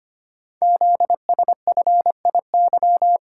Morse code, heard as a keyed tone, about 25 wpm, 700 Hz. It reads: ZSFIY
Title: Code Groups - 5 Letters - 25wpm